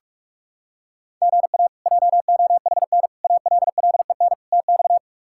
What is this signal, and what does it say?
Morse code, heard as a keyed tone, about 46 wpm, 700 Hz. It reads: GA JOHN ALLEN TX
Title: K1USN SST QSO Practice - 45wpm